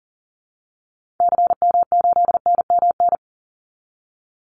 Morse code, heard as a keyed tone, about 40 wpm, 700 Hz. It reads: CM8NMN
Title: Call Signs Repeated 2x - 40wpm